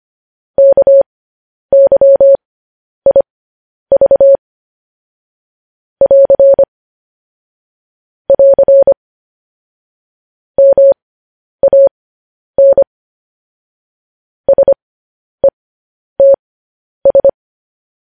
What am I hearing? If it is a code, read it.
Morse code, heard as a keyed tone, about 25 wpm, spaced as for 10 wpm, 550 Hz. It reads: KYIV <AR> <AR> MAN SETS
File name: podcast / News headlines-1x-25x10